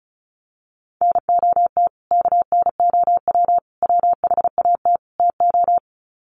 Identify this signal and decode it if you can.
Morse code, heard as a keyed tone, about 34 wpm, 700 Hz. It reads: NOT KNOW WHAT TO
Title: Sets of 4 Words 35wpm